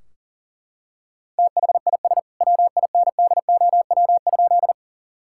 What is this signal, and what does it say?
Morse code, heard as a keyed tone, about 40 wpm, 700 Hz. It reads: THIS WINDOW?
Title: Sentences from Top 1600 Words 40wpm